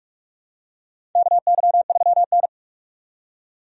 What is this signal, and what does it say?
Morse code, heard as a keyed tone, about 46 wpm, 700 Hz. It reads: KY3N
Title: USA Call Signs - 4 Characters - 45wpm